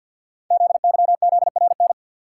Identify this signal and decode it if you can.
Morse code, heard as a keyed tone, about 51 wpm, 700 Hz. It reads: ZYZRN
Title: Code Groups - 5 Letters - 50wpm